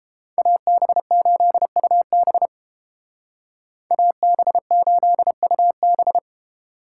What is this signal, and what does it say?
Morse code, heard as a keyed tone, about 29 wpm, 700 Hz. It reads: AB8UB AB8UB